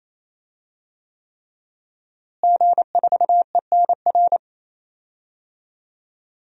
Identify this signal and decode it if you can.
Morse code, heard as a keyed tone, about 28 wpm, 700 Hz. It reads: G4ENR
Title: DXCC-100 Call Signs - 5 Characters - Repeated 3x - 28wpm